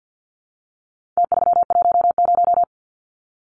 Morse code, heard as a keyed tone, about 50 wpm, 700 Hz. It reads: T310
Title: Numbers - Contest Style - 4-Digit 50wpm